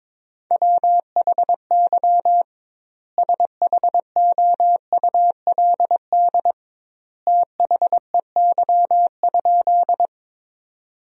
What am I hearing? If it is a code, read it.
Morse code, heard as a keyed tone, about 22 wpm, 700 Hz. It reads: WHY SHOULD THEY?
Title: 2x Word-Spacing - Sentences from Top 300 Words - 22wpm